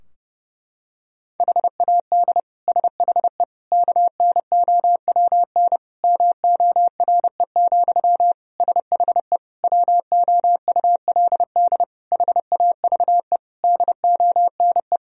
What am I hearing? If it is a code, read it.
Morse code, heard as a keyed tone, about 30 wpm, 700 Hz. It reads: HAD SHE KNOWN MORE, SHE WOULD HAVE DONE
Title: Sentences from Top 1000 Words - Review Part 3 - 30wpm